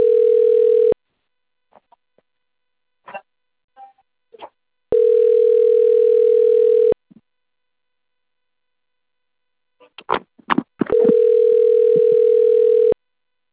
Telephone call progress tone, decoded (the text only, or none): ringback tone